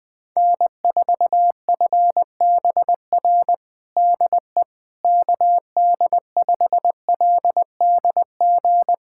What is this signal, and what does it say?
Morse code, heard as a keyed tone, about 20 wpm, 700 Hz. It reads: N4FBR DE KD5LDG
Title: USA Callsign DE Callsign 20wpm